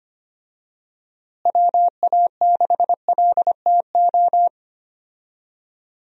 Morse code, heard as a keyed tone, about 25 wpm, 700 Hz. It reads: WA6LTO
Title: USA Call Signs - 6 Characters - Repeated 2x - 25wpm